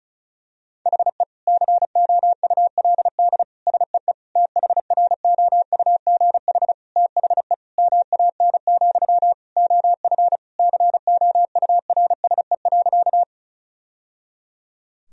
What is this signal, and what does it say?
Morse code, heard as a keyed tone, about 34 wpm, 700 Hz. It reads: HE COULD SEE THROUGH THE MAN, OF COURSE.
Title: Sentences from Top 1000 Words - Review Part 1 - 35wpm